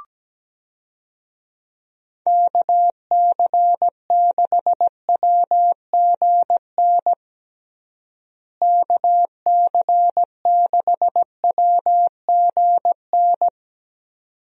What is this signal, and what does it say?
Morse code, heard as a keyed tone, about 17 wpm, 700 Hz. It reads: KC6WGN KC6WGN
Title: USA Call Signs - 6 Characters - Repeated 2x - 17wpm